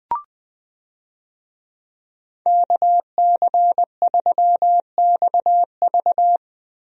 Morse code, heard as a keyed tone, about 20 wpm, 700 Hz. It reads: KC3XV
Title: USA Call Signs - 5 Characters - Repeated 2x - 20wpm